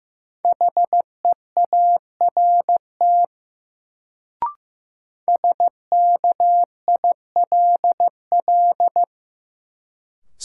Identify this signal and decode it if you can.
Morse code, heard as a keyed tone, about 16 wpm, 700 Hz. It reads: HEART SKILL
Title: Sentences from Top 1200 Words - New Word Drill - 15wpm